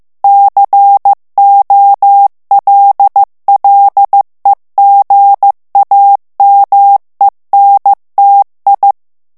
Morse code, heard as a keyed tone, about 15 wpm, 800 Hz. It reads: COLLEGAMENTI
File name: ITALIEN texte clair 1 vitesse 14 mots minute